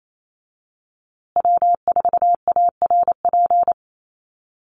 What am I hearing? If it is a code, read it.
Morse code, heard as a keyed tone, about 28 wpm, 700 Hz. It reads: W4ARP
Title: USA Call Signs Repeated 3x 28wpm